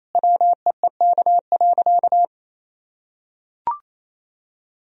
Morse code, heard as a keyed tone, about 28 wpm, 700 Hz. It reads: WEEK.
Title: Sentences from Top 1000 Words - Review Part 2 - 28wpm